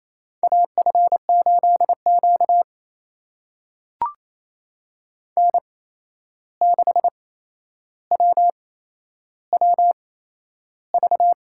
Morse code, heard as a keyed tone, about 28 wpm, spaced as for 11 wpm, 700 Hz. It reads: AF8Q N6WWV
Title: USA Call Signs - 8x Character Spacing - 28wpm